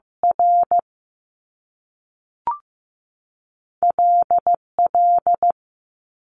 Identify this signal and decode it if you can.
Morse code, heard as a keyed tone, about 15 wpm, 700 Hz. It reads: R L L
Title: Top Two Letter Combinations in Words 15wpm